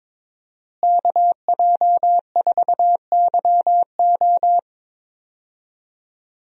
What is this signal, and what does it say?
Morse code, heard as a keyed tone, about 22 wpm, 700 Hz. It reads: KJ4YO
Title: USA Call Signs - 5 Characters - Repeated 2x - 22wpm